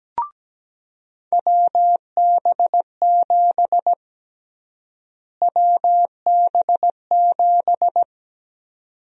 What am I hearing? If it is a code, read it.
Morse code, heard as a keyed tone, about 17 wpm, 700 Hz. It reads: WB7 WB7
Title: USA Call Sign Prefixes - Repeated 2x - 17wpm